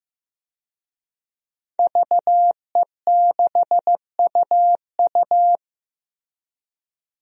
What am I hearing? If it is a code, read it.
Morse code, heard as a keyed tone, about 15 wpm, 700 Hz. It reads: VE6UU
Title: DXCC-100 Call Signs - 5 Characters - Repeated 2x - 15wpm